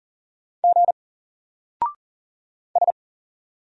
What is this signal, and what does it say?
Morse code, heard as a keyed tone, about 39 wpm, 700 Hz. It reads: G S
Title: Single Character (Letter:Number:Punctuation:Prosign) 40wpm